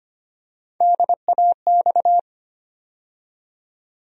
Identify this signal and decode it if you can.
Morse code, heard as a keyed tone, about 25 wpm, 700 Hz. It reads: DAX
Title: USA Call Sign Postfixes - 3 Letters 25wpm